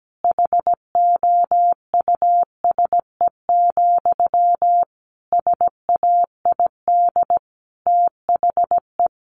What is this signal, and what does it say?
Morse code, heard as a keyed tone, about 17 wpm, 700 Hz. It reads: HOUSE, SAID THE
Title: Sentences from Top 400 Words 17wpm